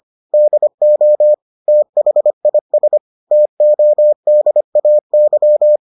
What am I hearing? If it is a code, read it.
Morse code, heard as a keyed tone, about 25 wpm, 600 Hz. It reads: DO THIS TODAY